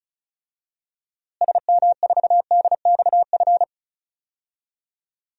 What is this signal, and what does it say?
Morse code, heard as a keyed tone, about 34 wpm, 700 Hz. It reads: SM4DXF
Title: DXCC-100 Call Signs - 6 Characters - Repeated 2x - 35wpm